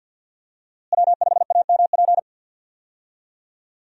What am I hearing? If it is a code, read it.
Morse code, heard as a keyed tone, about 50 wpm, 700 Hz. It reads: W5AMP
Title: USA Call Signs - 5 Characters - Repeated 2x - 50wpm